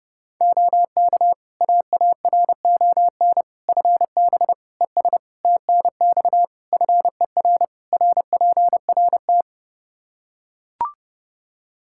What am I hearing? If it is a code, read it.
Morse code, heard as a keyed tone, about 30 wpm, 700 Hz. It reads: OK AARON FB ES TNX FER RPRT
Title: Practice QSO - Exchange 2 - Info - 30wpm